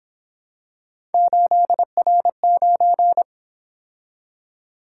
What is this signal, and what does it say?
Morse code, heard as a keyed tone, about 26 wpm, 700 Hz. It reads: 8R9
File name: Speed-Racing - Three Letter-Number Combinations - 22wpm